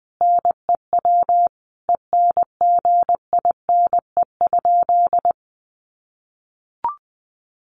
Morse code, heard as a keyed tone, about 20 wpm, 700 Hz. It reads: NEW ENGINE?
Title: Sentences from Top 1500 Words 20wpm